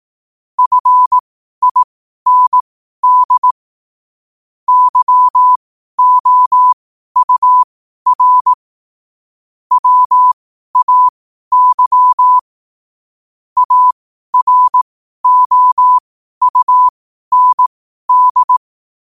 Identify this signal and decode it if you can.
Morse code, heard as a keyed tone, about 18 wpm, spaced as for 12 wpm, 1000 Hz. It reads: FIND YOUR WAY AROUND
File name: qotdfiles / 12wpm